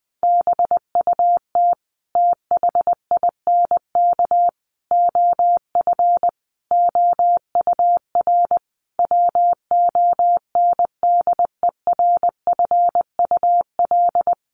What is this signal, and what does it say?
Morse code, heard as a keyed tone, about 20 wpm, 700 Hz. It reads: BUT THINK OF OUR WONDERFUL
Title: Sentences from Top 1100 Words 20wpm